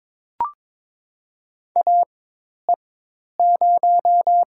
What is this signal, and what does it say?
Morse code, heard as a keyed tone, about 22 wpm, spaced as for 10 wpm, 700 Hz. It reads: AE0
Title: USA Call Signs - 5 Characters - 4x Character Spacing - 22wpm